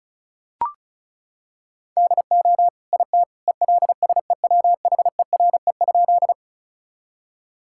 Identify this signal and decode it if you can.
Morse code, heard as a keyed tone, about 35 wpm, 700 Hz. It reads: DO IT ELSEWHERE?
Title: Sentences from Top 1600 Words 35wpm